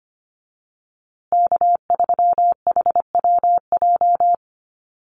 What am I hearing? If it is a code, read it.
Morse code, heard as a keyed tone, about 25 wpm, 700 Hz. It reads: K3HWJ
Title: USA Call Signs Repeated 3x 25wpm